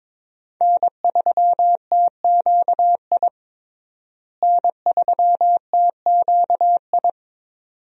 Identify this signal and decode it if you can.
Morse code, heard as a keyed tone, about 22 wpm, 700 Hz. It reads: N3TQI N3TQI
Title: USA Call Signs - 5 Characters - Repeated 2x - 22wpm